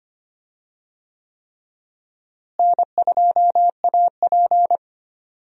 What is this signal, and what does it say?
Morse code, heard as a keyed tone, about 25 wpm, 700 Hz. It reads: N2AP
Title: USA Call Signs - 4 Characters - Repeated 3x - 25wpm